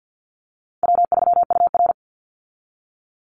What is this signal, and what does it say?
Morse code, heard as a keyed tone, about 50 wpm, 700 Hz. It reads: W3UR
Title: Easy Call Signs Repeated 3x - 50wpm